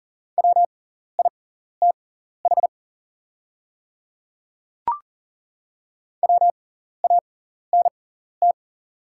Morse code, heard as a keyed tone, about 40 wpm, spaced as for 14 wpm, 690 Hz. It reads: WITH WANT
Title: 6x Character Spacing - Top 200 Words - 40wpm